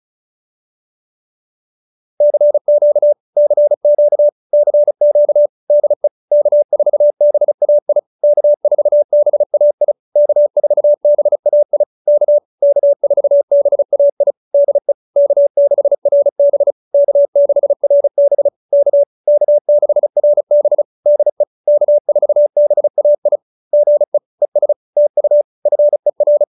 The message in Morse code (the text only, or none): CQ CQ CQ DE K4BAI K4BAI K4BAI K K4BAI DE K6RB K6RB K K6RB DE K4BAI GE ES TU FER